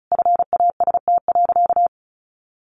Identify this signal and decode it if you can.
Morse code, heard as a keyed tone, about 34 wpm, 700 Hz. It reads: FAST.
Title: Sentences from Top 1100 Words 35wpm